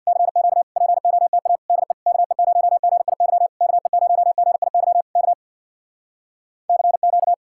Morse code, heard as a keyed tone, about 64 wpm, 700 Hz. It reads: CQ POTA DE KE0GIY KE0GIY K CQ